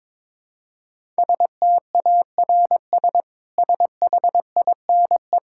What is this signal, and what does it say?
Morse code, heard as a keyed tone, about 22 wpm, 700 Hz. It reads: STARS SHINE
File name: practice / 22wpm / Easy Sentences with Hints - 22wpm